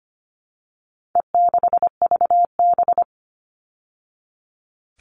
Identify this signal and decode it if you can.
Morse code, heard as a keyed tone, about 25 wpm, 700 Hz. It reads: E6VB
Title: All Cut Numbers - 4-Digit 25wpm